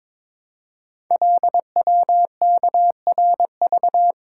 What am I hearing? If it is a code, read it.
Morse code, heard as a keyed tone, about 22 wpm, 700 Hz. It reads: LWKRV
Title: Code Groups - 5 Letters - 22wpm